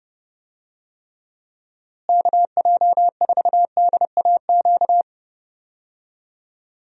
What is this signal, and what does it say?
Morse code, heard as a keyed tone, about 32 wpm, 700 Hz. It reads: KJ4DAQ
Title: USA Call Signs - 6 Characters - Repeated 3x - 30wpm